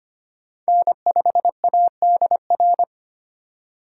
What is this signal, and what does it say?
Morse code, heard as a keyed tone, about 25 wpm, 700 Hz. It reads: N5ADR
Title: USA Call Signs - 5 Characters - 3x Character Spacing - 25wpm